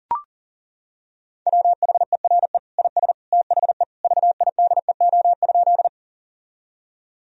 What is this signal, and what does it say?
Morse code, heard as a keyed tone, about 40 wpm, 700 Hz. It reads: WHERE IS THE VIDEO?
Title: Sentences from Top 1000 Words - Review Part 4 - 40wpm